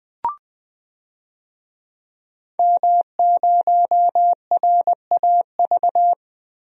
Morse code, heard as a keyed tone, about 20 wpm, 700 Hz. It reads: M0RAV
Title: DXCC-100 Call Signs - 5 Characters - 20wpm